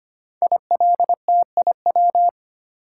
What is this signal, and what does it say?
Morse code, heard as a keyed tone, about 25 wpm, 700 Hz. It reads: ILTIW
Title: Code Groups - 5 Letters - 25wpm